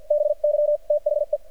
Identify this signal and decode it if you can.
Morse code, heard as a keyed tone, about 50 wpm, spaced as for 43 wpm, 590 Hz. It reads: BY THE
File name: archived / CRSnet